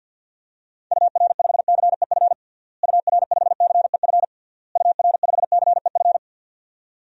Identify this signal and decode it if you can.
Morse code, heard as a keyed tone, about 50 wpm, 700 Hz. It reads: UR5CEF UR5CEF UR5CEF
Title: DXCC-100 Call Signs - 6 Characters - Repeated 3x - 50wpm